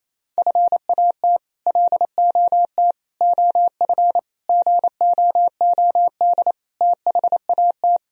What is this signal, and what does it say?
Morse code, heard as a keyed tone, about 27 wpm, 700 Hz. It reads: FAT LOT OF GOOD THAT
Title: Sentences from Top 1000 Words - Review Part 4 - 28wpm